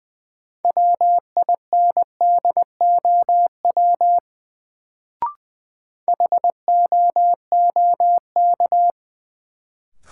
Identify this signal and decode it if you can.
Morse code, heard as a keyed tone, about 20 wpm, 700 Hz. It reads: WINDOW HOOK
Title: Sentences from Top 1600 Words - New Word Drill - 20wpm